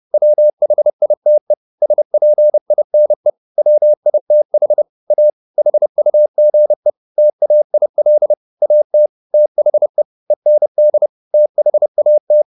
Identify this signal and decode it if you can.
Morse code, heard as a keyed tone, about 30 wpm, 600 Hz. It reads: WHITE SPINE WITH A HUGE TAIL AT THE END THAT